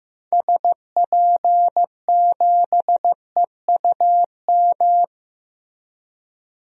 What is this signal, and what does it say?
Morse code, heard as a keyed tone, about 16 wpm, 700 Hz. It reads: SP7EUM
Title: DXCC-100 Call Signs - 6 Characters - Repeated 2x - 15wpm